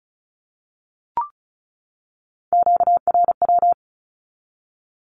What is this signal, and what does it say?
Morse code, heard as a keyed tone, about 34 wpm, 700 Hz. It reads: QRW
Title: Q-Codes 35wpm